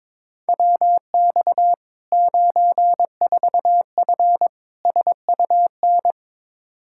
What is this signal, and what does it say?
Morse code, heard as a keyed tone, about 22 wpm, 700 Hz. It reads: WX 94F SUN
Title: Practice QSO - Exchange 2 - Info - 22wpm